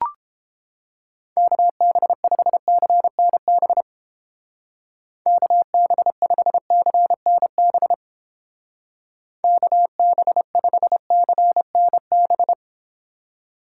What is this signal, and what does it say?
Morse code, heard as a keyed tone, about 30 wpm, 700 Hz. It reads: KB5CNB KB5CNB KB5CNB